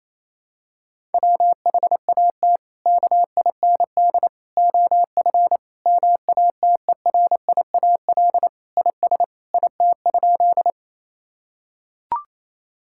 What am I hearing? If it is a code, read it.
Morse code, heard as a keyed tone, about 28 wpm, 700 Hz. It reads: WHAT KIND OF MATERIAL IS IT?
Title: Sentences from Top 1000 Words - Review Part 5 - 28wpm